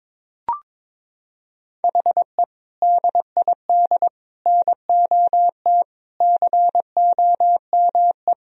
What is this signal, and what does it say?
Morse code, heard as a keyed tone, about 22 wpm, 700 Hz. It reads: HE DID NOT COME
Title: Sentences from Top 1000 Words - Review Part 3 - 22wpm